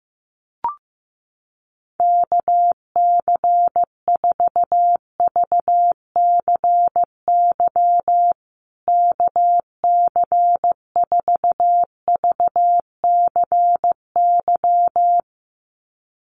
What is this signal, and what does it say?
Morse code, heard as a keyed tone, about 15 wpm, 700 Hz. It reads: KC4VCY KC4VCY
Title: USA Call Signs Repeated 2x 15wpm